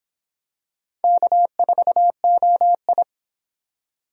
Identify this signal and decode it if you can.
Morse code, heard as a keyed tone, about 25 wpm, 700 Hz. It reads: K4OI